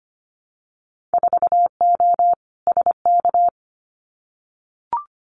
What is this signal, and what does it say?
Morse code, heard as a keyed tone, about 25 wpm, 690 Hz. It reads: 4O SK
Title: Winter Field Day Exchanges Encore 25wpm